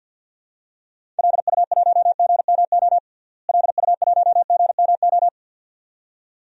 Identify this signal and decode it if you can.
Morse code, heard as a keyed tone, about 48 wpm, 700 Hz. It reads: LU1GMO LU1GMO
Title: DXCC-100 Call Signs - 6 Characters - Repeated 2x - 50wpm